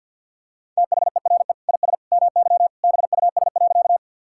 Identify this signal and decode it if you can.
Morse code, heard as a keyed tone, about 50 wpm, 700 Hz. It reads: THERE IS MY BUS.
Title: Sentences from Top 1000 Words - Review Part 6 - 50wpm